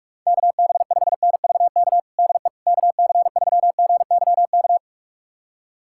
Morse code, heard as a keyed tone, about 41 wpm, 700 Hz. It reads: KB5NVK DE KC3GYK
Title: USA Callsign DE Callsign 45wpm